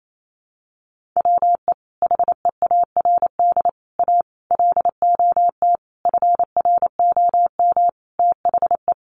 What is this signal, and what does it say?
Morse code, heard as a keyed tone, about 28 wpm, 700 Hz. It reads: WE HEARD A LOT FROM THE
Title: Sentences from Top 1100 Words 28wpm